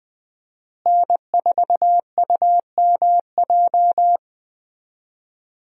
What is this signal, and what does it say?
Morse code, heard as a keyed tone, about 20 wpm, 700 Hz. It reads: N4UMJ
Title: USA Call Signs - 5 Characters - Repeated 2x - 20wpm